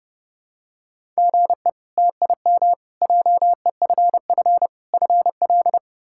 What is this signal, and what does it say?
Morse code, heard as a keyed tone, about 29 wpm, 700 Hz. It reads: GE TIM JEFF FL
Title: K1USN SST QSO Practice - 30wpm